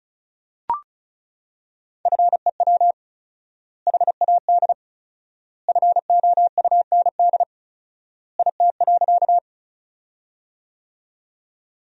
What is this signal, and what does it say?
Morse code, heard as a keyed tone, about 36 wpm, 700 Hz. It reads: FEW HAD FOUND IT.
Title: 4x Word-Spacing - Sentences from Top 300 Words - 35wpm